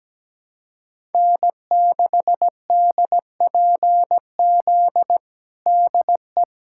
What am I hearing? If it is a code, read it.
Morse code, heard as a keyed tone, about 17 wpm, 700 Hz. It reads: N6DPZ DE
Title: USA Callsign DE Callsign - 5 Characters - 17wpm